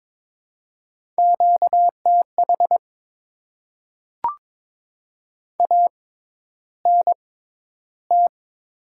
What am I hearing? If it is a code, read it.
Morse code, heard as a keyed tone, about 22 wpm, spaced as for 11 wpm, 700 Hz. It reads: QTH ANT
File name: Top 20 QSO Elements - 6x Character Spacing - 22wpm